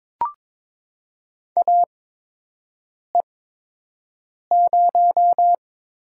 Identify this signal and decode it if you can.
Morse code, heard as a keyed tone, about 22 wpm, spaced as for 6 wpm, 700 Hz. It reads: AE0
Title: USA Call Signs - 5 Characters - 8x Character Spacing - 22wpm